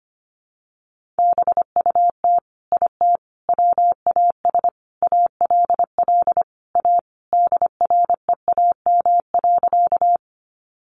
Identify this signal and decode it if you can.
Morse code, heard as a keyed tone, about 25 wpm, 700 Hz. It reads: BUT IT WAS ALL A DREAM.
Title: Sentences from Top 1500 Words 25wpm